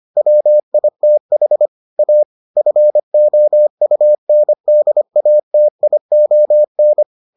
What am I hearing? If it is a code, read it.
Morse code, heard as a keyed tone, about 25 wpm, 600 Hz. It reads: WITH A FOUNDATION